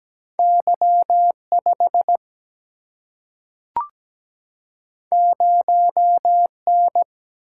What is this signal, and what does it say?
Morse code, heard as a keyed tone, about 17 wpm, 700 Hz. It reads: Y5 0N
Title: Single-Letter Single-Number 17wpm